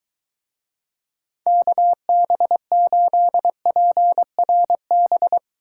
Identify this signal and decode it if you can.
Morse code, heard as a keyed tone, about 23 wpm, 700 Hz. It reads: KB8PRB